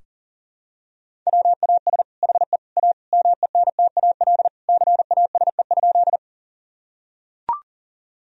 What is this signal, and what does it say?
Morse code, heard as a keyed tone, about 40 wpm, 700 Hz. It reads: WAS HE A MENTAL CASE?
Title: Sentences from Top 1000 Words - Review Part 4 - 40wpm